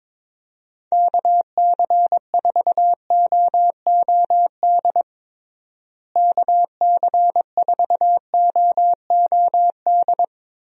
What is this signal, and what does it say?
Morse code, heard as a keyed tone, about 22 wpm, 700 Hz. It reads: KC4OOD KC4OOD
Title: USA Call Signs - 6 Characters - Repeated 3x - 22wpm